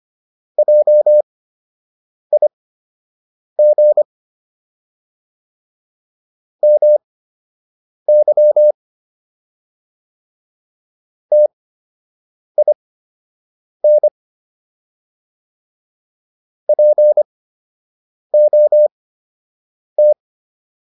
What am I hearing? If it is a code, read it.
Morse code, heard as a keyed tone, about 25 wpm, spaced as for 7 wpm, 600 Hz. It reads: JIG MY TIN POT